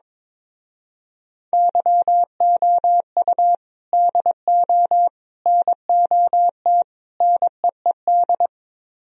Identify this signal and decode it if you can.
Morse code, heard as a keyed tone, about 22 wpm, 700 Hz. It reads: YOU DO NOT NEED
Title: Speed-Racing - Sets of 4 Words 22wpm